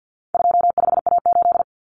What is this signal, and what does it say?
Morse code, heard as a keyed tone, about 50 wpm, 700 Hz. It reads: 25A8
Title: Common Cut Numbers - 4-Digit 50wpm